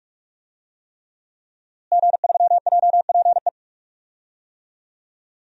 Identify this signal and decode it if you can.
Morse code, heard as a keyed tone, about 46 wpm, 700 Hz. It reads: G3JPE
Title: DXCC-100 Call Signs - 5 Characters - Repeated 3x - 45wpm